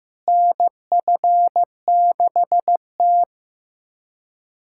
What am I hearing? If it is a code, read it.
Morse code, heard as a keyed tone, about 15 wpm, 700 Hz. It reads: NF6T
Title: USA Callsign DE Callsign - 4 Characters - 15wpm